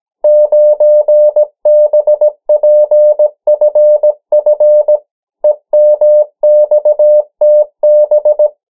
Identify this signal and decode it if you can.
Morse code, heard as a keyed tone, about 17 wpm, 600 Hz. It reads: 9BPFF EMXTB
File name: morse mp3 / 14 urok 13